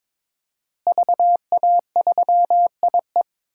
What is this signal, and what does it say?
Morse code, heard as a keyed tone, about 21 wpm, 700 Hz. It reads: VA3IE
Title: DXCC-100 Call Signs - 5 Characters - Repeated 2x - 22wpm